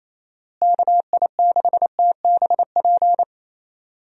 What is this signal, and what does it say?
Morse code, heard as a keyed tone, about 28 wpm, 700 Hz. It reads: KI6TBP
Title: USA Call Signs - 6 Characters - Repeated 2x - 28wpm